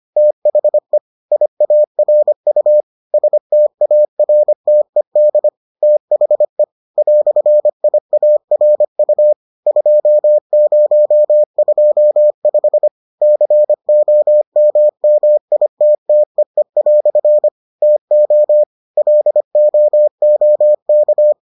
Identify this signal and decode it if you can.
Morse code, heard as a keyed tone, about 25 wpm, 600 Hz. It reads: THE IARU STARTED THE "IARU 2025 COMMITTEE" TO LOOK